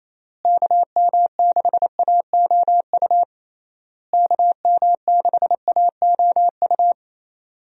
Morse code, heard as a keyed tone, about 27 wpm, 700 Hz. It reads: KM6AOU KM6AOU
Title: USA Call Signs - 6 Characters - Repeated 3x - 28wpm